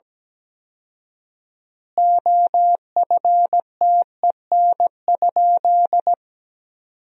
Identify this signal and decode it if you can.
Morse code, heard as a keyed tone, about 17 wpm, 700 Hz. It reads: OFTEN?
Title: 4x Word-Spacing - Sentences from Top 400 Words - 17wpm